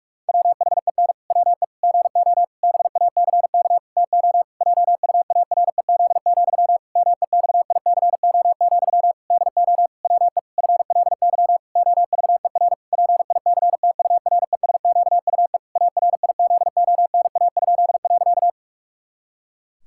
Morse code, heard as a keyed tone, about 47 wpm, 700 Hz. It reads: WHEN WE GO BACK TO JUAREZ, MEXICO, DO WE FLY OVER PICTURESQUE ARIZONA?.
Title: Pangram Challenge v2 - 45wpm